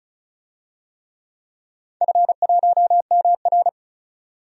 Morse code, heard as a keyed tone, about 36 wpm, 700 Hz. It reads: F1MR
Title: DXCC-100 Call Signs - 4 Characters - 35wpm